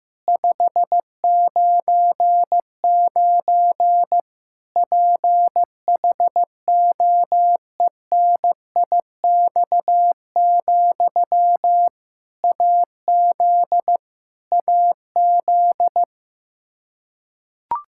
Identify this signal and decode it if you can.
Morse code, heard as a keyed tone, about 15 wpm, 700 Hz. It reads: 599 PHOENIX, AZ AZ
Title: K1USN SST QSO Practice 15wpm